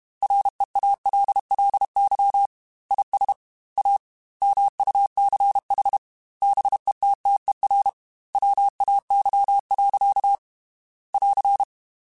Morse code, heard as a keyed tone, about 32 wpm, 800 Hz. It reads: REALLY IS A MUCH BETTER WAY. <AR>